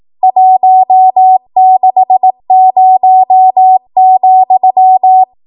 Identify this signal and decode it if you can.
Morse code, heard as a keyed tone, about 18 wpm, 750 Hz. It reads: 160,
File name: Morse / 18WPM